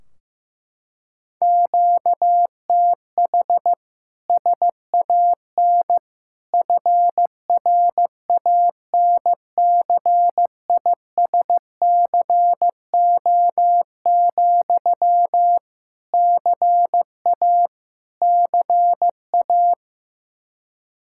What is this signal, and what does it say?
Morse code, heard as a keyed tone, about 15 wpm, 700 Hz. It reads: QTH SAN FRANCISCO, CA CA
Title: K1USN SST QSO Practice 15wpm